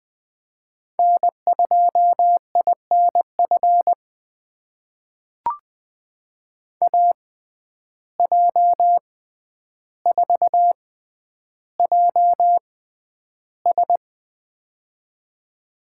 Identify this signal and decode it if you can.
Morse code, heard as a keyed tone, about 20 wpm, spaced as for 10 wpm, 700 Hz. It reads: N2INF AJ4JS
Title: USA Call Signs - 5 Characters - 6x Character Spacing - 20wpm